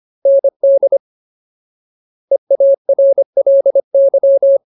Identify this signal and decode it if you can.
Morse code, heard as a keyed tone, about 25 wpm, 550 Hz. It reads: ND EARLY
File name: Ultimate Word List / 25WPM Ultimate